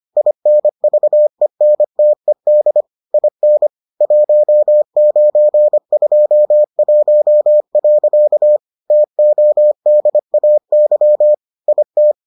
Morse code, heard as a keyed tone, about 25 wpm, 600 Hz. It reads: INVENTED IN 1921. TODAY IT